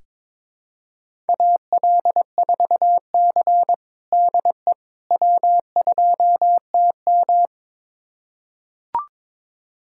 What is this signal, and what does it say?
Morse code, heard as a keyed tone, about 22 wpm, 700 Hz. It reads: AL4C DE W2TM
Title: USA Callsign DE Callsign - 4 Characters - 22wpm